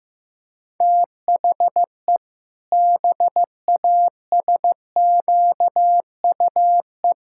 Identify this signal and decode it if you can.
Morse code, heard as a keyed tone, about 14 wpm, 700 Hz. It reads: THE BASQUE
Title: Pangram Challenge v2 - 15wpm